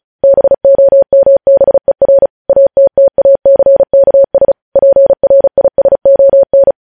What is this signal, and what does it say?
Morse code, heard as a keyed tone, about 36 wpm, 550 Hz. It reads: BOMBER ATTACKS PRISON
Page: News Headlines in Morse Code at 35 WPM